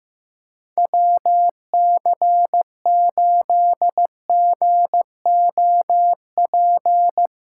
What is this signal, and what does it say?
Morse code, heard as a keyed tone, about 15 wpm, 700 Hz. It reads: WC8GOP
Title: USA Callsign DE Callsign 15wpm